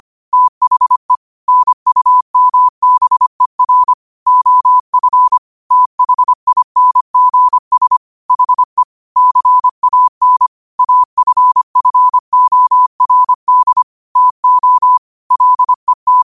Morse code, heard as a keyed tone, about 25 wpm, 1000 Hz. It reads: THE NUMBER OF THINGS HE CAN AFFORD TO LET
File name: qotdfiles / 25wpm